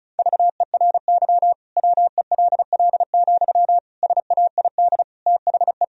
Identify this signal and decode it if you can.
Morse code, heard as a keyed tone, about 34 wpm, 700 Hz. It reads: VERY WELL, SAID THE
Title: Sentences from Top 1000 Words - Review Part 4 - 35wpm